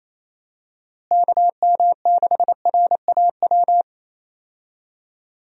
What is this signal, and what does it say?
Morse code, heard as a keyed tone, about 28 wpm, 700 Hz. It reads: KM6RAW
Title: USA Call Signs - 6 Characters - Repeated 3x - 28wpm